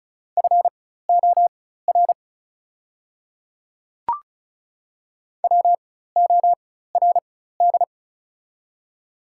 Morse code, heard as a keyed tone, about 36 wpm, spaced as for 16 wpm, 690 Hz. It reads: FOR WORD
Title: 4x Character Spacing - Top 100 Words - 35wpm